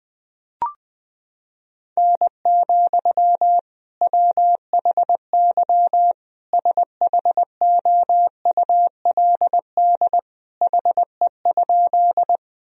Morse code, heard as a keyed tone, about 21 wpm, 700 Hz. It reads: N, WHY SHOULD HE?
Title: Sentences from Top 300 Words and Abbreviations 20wpm